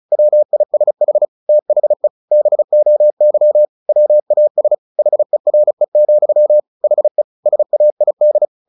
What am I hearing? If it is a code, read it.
Morse code, heard as a keyed tone, about 36 wpm, 600 Hz. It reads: WISH THE BOY WAS HERE, HE SAID
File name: the old man and the sea / the old man and the sea 1 35wpm